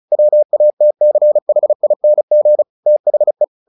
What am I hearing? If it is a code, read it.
Morse code, heard as a keyed tone, about 34 wpm, 600 Hz. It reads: WATCHING THE
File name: the old man and the sea / the old man and the sea 1 35wpm